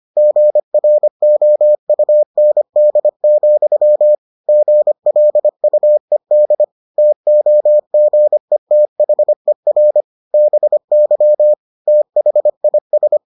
Morse code, heard as a keyed tone, about 25 wpm, 600 Hz. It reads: GROUND, GLUED TOGETHER BY THIS